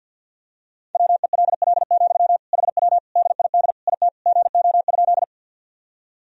Morse code, heard as a keyed tone, about 51 wpm, 700 Hz. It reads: WELL, HW DID IT GO?
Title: Sentences from Top 200 Words and Abbreviations 50wpm